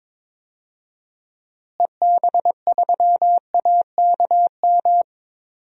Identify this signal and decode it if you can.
Morse code, heard as a keyed tone, about 22 wpm, 700 Hz. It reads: EB3AKM
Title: DXCC-100 Call Signs - 6 Characters - Repeated 3x - 22wpm